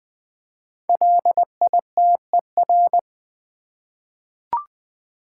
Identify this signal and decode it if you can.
Morse code, heard as a keyed tone, about 20 wpm, 700 Hz. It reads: LITER
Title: 5 Letter Words - First 3 Letters Given - 20wpm